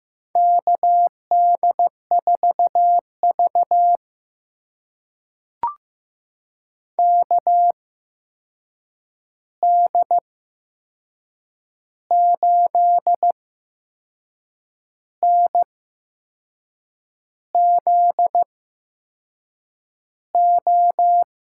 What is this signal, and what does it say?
Morse code, heard as a keyed tone, about 15 wpm, spaced as for 6 wpm, 700 Hz. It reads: KD4V KD8NZO
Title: USA Call Signs - 8x Character Spacing - 15wpm